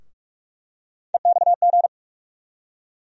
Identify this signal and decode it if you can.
Morse code, heard as a keyed tone, about 46 wpm, 700 Hz. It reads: EXG
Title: USA Call Sign Postfixes - 3 Letters 45wpm